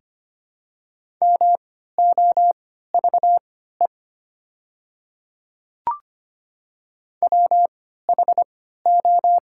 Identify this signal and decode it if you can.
Morse code, heard as a keyed tone, about 25 wpm, spaced as for 14 wpm, 690 Hz. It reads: MOVE WHO
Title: 3x Character Spacing - Top 200 Words - 25wpm